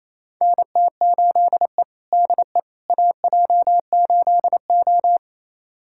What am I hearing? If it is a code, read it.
Morse code, heard as a keyed tone, about 28 wpm, 700 Hz. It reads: NT8E DE AJ8O
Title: USA Callsign DE Callsign - 4 Characters - 28wpm